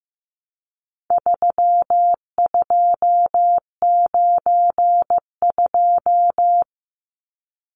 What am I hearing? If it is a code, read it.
Morse code, heard as a keyed tone, about 15 wpm, 700 Hz. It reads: 3292
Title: Numbers - 4-Digit 15wpm